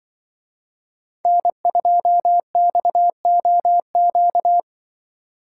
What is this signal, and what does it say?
Morse code, heard as a keyed tone, about 24 wpm, 700 Hz. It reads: N2XOQ
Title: Speed Racing - USA Call Signs 20wpm